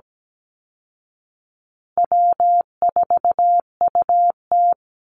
Morse code, heard as a keyed tone, about 17 wpm, 700 Hz. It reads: W4UT
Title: Easy Call Signs Repeated 2x - 17wpm